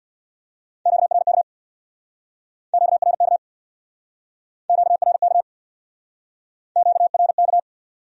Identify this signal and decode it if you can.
Morse code, heard as a keyed tone, about 64 wpm, 700 Hz. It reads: QRK QRK QRK QRK